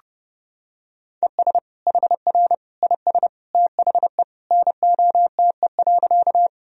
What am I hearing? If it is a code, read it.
Morse code, heard as a keyed tone, about 32 wpm, 700 Hz. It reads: ES HR IS THE NOTE.
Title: Sentences from Top 500 Words and Abbreviations 30wpm